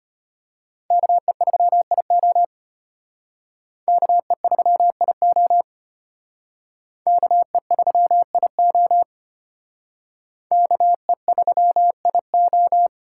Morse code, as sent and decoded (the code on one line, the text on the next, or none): -.- . ...-- .. --- / -.- . ...-- .. --- / -.- . ...-- .. --- / -.- . ...-- .. ---
KE3IO KE3IO KE3IO KE3IO